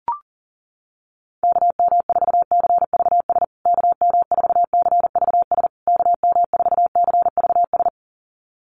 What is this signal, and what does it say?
Morse code, heard as a keyed tone, about 40 wpm, 700 Hz. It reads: KM4CVS KM4CVS KM4CVS
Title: USA Call Signs Repeated 3x 40wpm